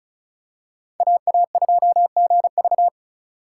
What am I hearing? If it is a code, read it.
Morse code, heard as a keyed tone, about 34 wpm, 700 Hz. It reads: AA2GV
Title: USA Call Signs - 5 Characters - 2x Character Spacing - 35wpm